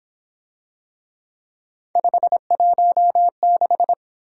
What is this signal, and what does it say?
Morse code, heard as a keyed tone, about 26 wpm, 700 Hz. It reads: 516
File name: Speed-Racing - Three Letter-Number Combinations - 22wpm